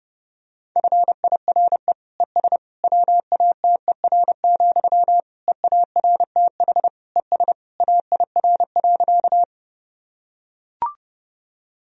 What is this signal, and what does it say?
Morse code, heard as a keyed tone, about 30 wpm, 700 Hz. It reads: FIRE ES WATER, EARTH ES AIR.
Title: Sentences from Top 400 Words and Abbreviations 30wpm